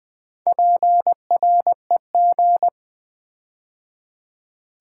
Morse code, heard as a keyed tone, about 20 wpm, 700 Hz. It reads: PREG
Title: Build A Word - 4 Letters At A Time - 8-Letter Words - 20wpm